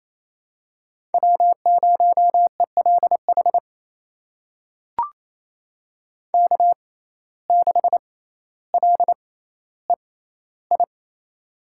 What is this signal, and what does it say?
Morse code, heard as a keyed tone, about 28 wpm, spaced as for 14 wpm, 700 Hz. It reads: W0ELH K6LEI
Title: USA Call Signs - 5 Characters - 6x Character Spacing - 28wpm